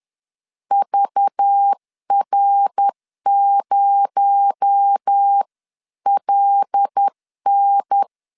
Morse code, heard as a keyed tone, about 10 wpm, 800 Hz. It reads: VR0LN
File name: LEZIONI / 7 GF9Z / 7 GF 10wpm 2514